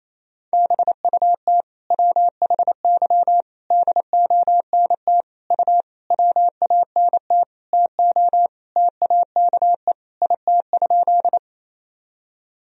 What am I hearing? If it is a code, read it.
Morse code, heard as a keyed tone, about 28 wpm, 700 Hz. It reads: BUT WHY DONT U WANT TO TAKE IT?
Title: Sentences from Top 300 Words and Abbreviations 28wpm